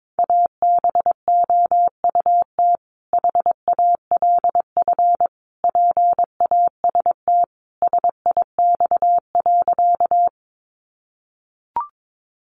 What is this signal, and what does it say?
Morse code, heard as a keyed tone, about 22 wpm, 700 Hz. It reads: ABOUT HALF PAST SIX.
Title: Sentences from Top 600 Words 22wpm